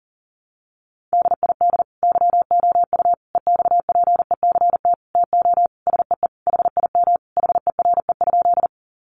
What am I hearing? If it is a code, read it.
Morse code, heard as a keyed tone, about 40 wpm, 700 Hz. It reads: DID YOU EXPECT TO SEE HIM HERE?
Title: Sentences from Top 1100 Words 40wpm